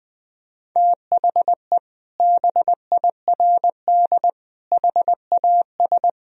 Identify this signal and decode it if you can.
Morse code, heard as a keyed tone, about 20 wpm, 700 Hz. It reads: THE BIRD HAS
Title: Sentences from Top 1000 Words - Review Part 2 - 20wpm